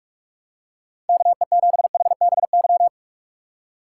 Morse code, heard as a keyed tone, about 46 wpm, 700 Hz. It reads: KE7HBY
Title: USA Call Signs - 2x Character Spacing - 45wpm